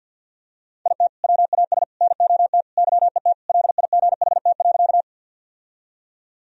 Morse code, heard as a keyed tone, about 48 wpm, 700 Hz. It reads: IT WAS NOT YET LIGHT.
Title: Sentences from Top 1000 Words - Review Part 1 - 50wpm